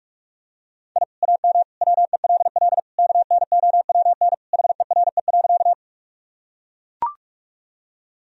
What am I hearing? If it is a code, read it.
Morse code, heard as a keyed tone, about 42 wpm, 700 Hz. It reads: I AM WELL KNOWN HERE.
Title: Sentences from Top 1000 Words - Review Part 3 - 45wpm